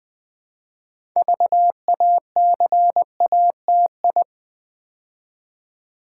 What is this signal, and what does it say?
Morse code, heard as a keyed tone, about 21 wpm, 700 Hz. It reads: VACATI
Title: Build A Word - 2 Letters At A Time - 8-Letter Words - 20wpm